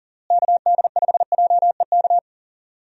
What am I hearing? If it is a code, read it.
Morse code, heard as a keyed tone, about 40 wpm, 700 Hz. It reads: KD5JEK
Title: USA Call Signs - 6 Characters - Repeated 3x - 40wpm